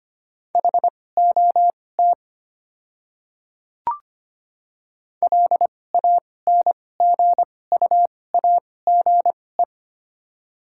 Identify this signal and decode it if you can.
Morse code, heard as a keyed tone, about 25 wpm, spaced as for 18 wpm, 700 Hz. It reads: HOT LANGUAGE
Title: 2x Character Spacing - Top 500 Words - 25wpm